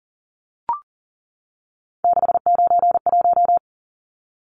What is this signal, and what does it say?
Morse code, heard as a keyed tone, about 40 wpm, 700 Hz. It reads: 691
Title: Numbers - 3-Digit 40wpm